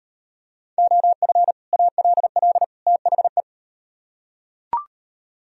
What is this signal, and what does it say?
Morse code, heard as a keyed tone, about 38 wpm, 700 Hz. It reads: OF ALL THE